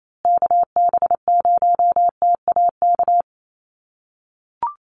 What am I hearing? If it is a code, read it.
Morse code, heard as a keyed tone, about 28 wpm, 700 Hz. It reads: KB0TAK
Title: USA Call Signs Repeated 3x 28wpm